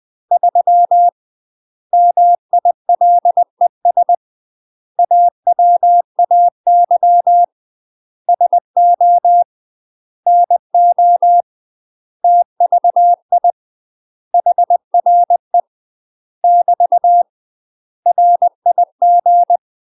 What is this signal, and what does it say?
Morse code, heard as a keyed tone, about 20 wpm, 700 Hz. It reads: 3 MILES AWAY SO NO TVI HRE <BT> RIG
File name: UK MASTER QSOs TRAINING / 20WPM / UK MASTER QSOs 0001